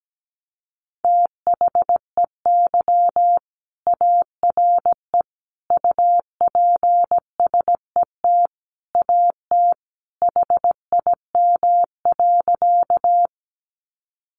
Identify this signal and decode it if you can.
Morse code, heard as a keyed tone, about 17 wpm, 700 Hz. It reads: THEY ARE UPSET AT HIM.
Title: Sentences from Top 800 Words 17wpm